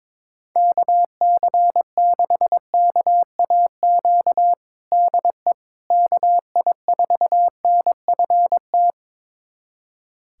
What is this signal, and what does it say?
Morse code, heard as a keyed tone, about 22 wpm, 700 Hz. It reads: KC6KAQ DE KI4NFT
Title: USA Callsign DE Callsign - 6 Characters - 22wpm